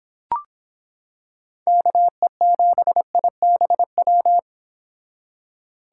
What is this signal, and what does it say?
Morse code, heard as a keyed tone, about 25 wpm, 700 Hz. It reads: KE7IBW